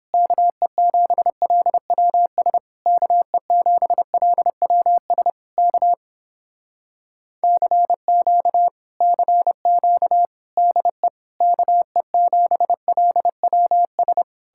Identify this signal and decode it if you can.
Morse code, heard as a keyed tone, about 29 wpm, 700 Hz. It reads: KE7LWS KE7LWS K CQ CQ DE KE7LWS